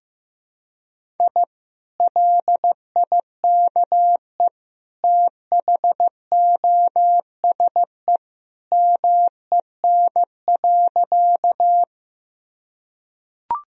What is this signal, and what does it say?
Morse code, heard as a keyed tone, about 15 wpm, 700 Hz. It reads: I LIKE THOSE MEN.
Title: Sentences from Top 1000 Words - Review Part 1 - 15wpm